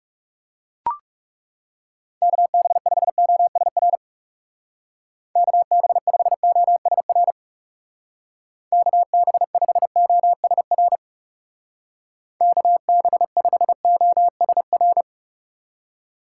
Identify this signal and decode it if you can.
Morse code, heard as a keyed tone, about 39 wpm, 700 Hz. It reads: KB5OSR KB5OSR KB5OSR KB5OSR